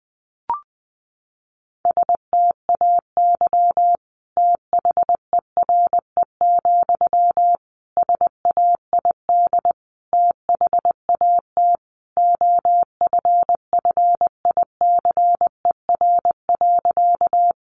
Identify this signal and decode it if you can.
Morse code, heard as a keyed tone, about 20 wpm, 700 Hz. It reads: STAY THERE, SAID THAT OFFICER.
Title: Sentences from Top 1200 Words 20wpm